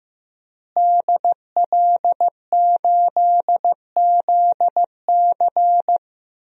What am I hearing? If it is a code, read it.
Morse code, heard as a keyed tone, about 15 wpm, 700 Hz. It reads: DL8ZC
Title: DXCC-100 Call Signs - 5 Characters - Repeated 3x - 15wpm